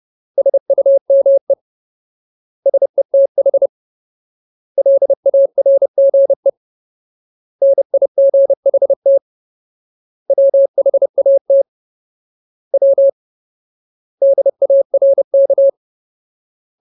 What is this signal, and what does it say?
Morse code, heard as a keyed tone, about 30 wpm, 550 Hz. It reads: SUME SETH LARGE NIGHT WHAT W DARK
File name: Ultimate Word List / 30WPM Ultimate